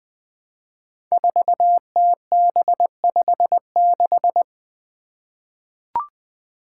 Morse code, heard as a keyed tone, about 20 wpm, 700 Hz. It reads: 4TB56
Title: Five Challenging Letter-Number Combinations 20wpm